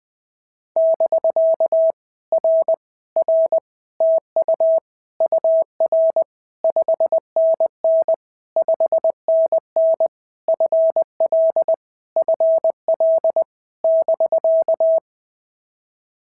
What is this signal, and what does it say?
Morse code, heard as a keyed tone, about 20 wpm, 650 Hz. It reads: <BK> R R TU UR 5NN 5NN FL FL <BK>
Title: POTA QSO Practice - 20wpm